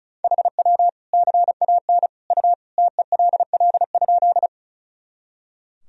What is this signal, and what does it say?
Morse code, heard as a keyed tone, about 36 wpm, 700 Hz. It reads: HW CAN U TELL?
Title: Sentences from Top 200 Words and Abbreviations 35wpm